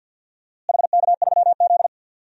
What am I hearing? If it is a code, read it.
Morse code, heard as a keyed tone, about 53 wpm, 700 Hz. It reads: HK3Z
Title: DXCC-100 Call Signs - 4 Characters - Repeated 3x - 50wpm